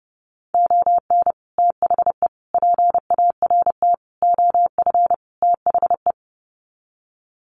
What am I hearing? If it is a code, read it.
Morse code, heard as a keyed tone, about 30 wpm, 700 Hz. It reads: ON THE PART OF THE
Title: Sets of 5 Words 30wpm